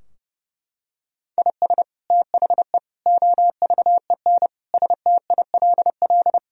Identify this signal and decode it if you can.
Morse code, heard as a keyed tone, about 32 wpm, 700 Hz. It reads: IS THE OVEN STILL
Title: Sentences from Top 1000 Words - Review Part 4 - 30wpm